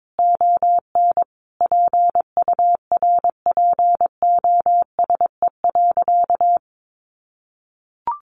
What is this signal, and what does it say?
Morse code, heard as a keyed tone, about 22 wpm, 700 Hz. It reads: ON PURPOSE.
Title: Sentences from Top 800 Words 22wpm